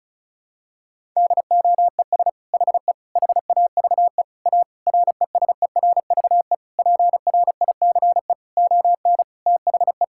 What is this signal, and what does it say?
Morse code, heard as a keyed tone, about 36 wpm, 700 Hz. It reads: DOES HE HAVE A RESERVE PRICE ON THE
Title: Sentences from Top 1000 Words - Review Part 6 - 35wpm